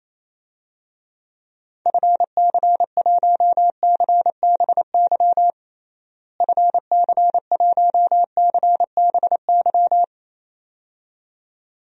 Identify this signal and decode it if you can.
Morse code, heard as a keyed tone, about 28 wpm, 700 Hz. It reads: FC1CBY FC1CBY
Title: DXCC-100 Call Signs - 6 Characters - Repeated 2x - 28wpm